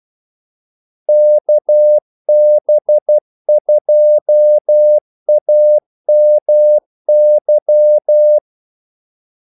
Morse code, heard as a keyed tone, about 12 wpm, 600 Hz. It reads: KB2AMY
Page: 2 ekstra ordmellemrum mellem kaldesignalerne